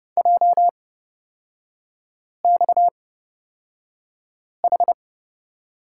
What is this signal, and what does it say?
Morse code, heard as a keyed tone, about 29 wpm, spaced as for 5 wpm, 690 Hz. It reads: JXH
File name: Lesson 37e - Letter x - Three Characters 30wpm